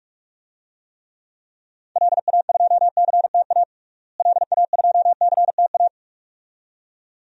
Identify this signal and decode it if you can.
Morse code, heard as a keyed tone, about 41 wpm, 700 Hz. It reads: LA2CTA LA2CTA
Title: DXCC-100 Call Signs - 6 Characters - Repeated 2x - 45wpm